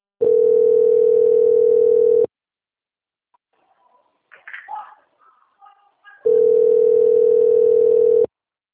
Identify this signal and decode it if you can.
Ringback tone (North America), 440 + 480 Hz.
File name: out- - - - - .3262